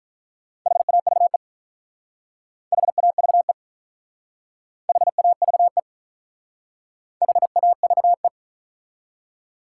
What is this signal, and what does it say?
Morse code, heard as a keyed tone, about 44 wpm, 700 Hz. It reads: HAVE HAVE HAVE HAVE